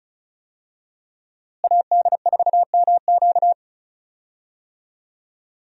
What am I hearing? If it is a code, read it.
Morse code, heard as a keyed tone, about 36 wpm, 700 Hz. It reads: AD4MQ
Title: USA Call Signs - 5 Characters - Repeated 3x - 35wpm